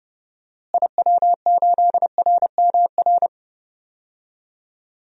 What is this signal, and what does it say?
Morse code, heard as a keyed tone, about 30 wpm, 700 Hz. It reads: IW8RMR
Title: DXCC-100 Call Signs - 6 Characters - Repeated 3x - 30wpm